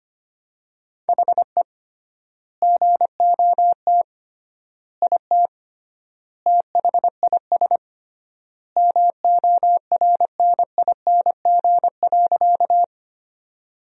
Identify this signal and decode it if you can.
Morse code, heard as a keyed tone, about 25 wpm, 700 Hz. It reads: HE GOT IT THIS MORNING.
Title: 3x Word-Spacing - Sentences from Top 400 Words - 25wpm